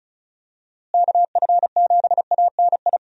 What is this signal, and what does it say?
Morse code, heard as a keyed tone, about 36 wpm, 700 Hz. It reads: KF7ANI
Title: USA Call Signs - 6 Characters - Repeated 2x - 35wpm